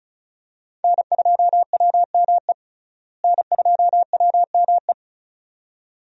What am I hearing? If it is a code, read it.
Morse code, heard as a keyed tone, about 36 wpm, 700 Hz. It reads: N2WME N2WME
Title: USA Call Signs - 5 Characters - Repeated 2x - 35wpm